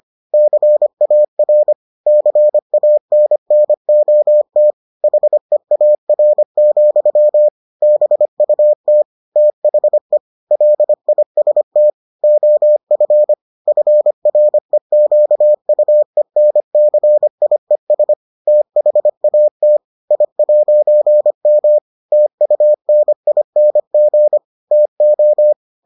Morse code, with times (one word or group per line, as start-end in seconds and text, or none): 0.3-1.7 CAR
2.1-4.7 CANNOT
5.0-7.5 HEAR,
7.8-9.0 BUT
9.4-10.2 THE
10.5-11.9 LIST
12.2-13.3 OF
13.7-18.1 FREQUENCIES
18.5-19.8 THAT
20.1-21.8 I'M
22.1-24.4 TUNING
24.7-25.5 TO